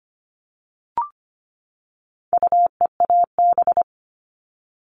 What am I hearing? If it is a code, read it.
Morse code, heard as a keyed tone, about 25 wpm, 700 Hz. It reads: UEAB
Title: All Cut Numbers - 4-Digit 25wpm